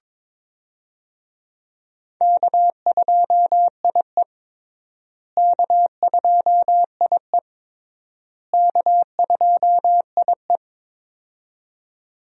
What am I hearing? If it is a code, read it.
Morse code, heard as a keyed tone, about 22 wpm, 700 Hz. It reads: K2IE K2IE K2IE
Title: USA Call Signs - 4 Characters - Repeated 3x - 22wpm